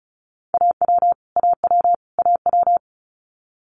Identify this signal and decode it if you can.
Morse code, heard as a keyed tone, about 36 wpm, 700 Hz. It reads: AW AW AW
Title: Multi-letter Phonemes Repeated 3x 35wpm